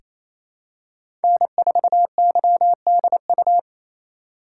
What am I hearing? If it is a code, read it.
Morse code, heard as a keyed tone, about 27 wpm, 700 Hz. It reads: N4YDU
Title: K1USN SST QSO Practice - 28wpm